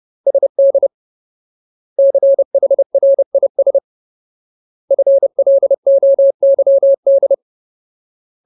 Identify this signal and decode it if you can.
Morse code, heard as a keyed tone, about 30 wpm, 550 Hz. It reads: SD CHRIS FLOYD
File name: Ultimate Word List / 30WPM Ultimate